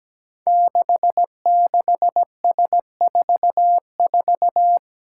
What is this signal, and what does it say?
Morse code, heard as a keyed tone, about 17 wpm, 700 Hz. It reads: 66S44
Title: Five Challenging Letter-Number Combinations 17wpm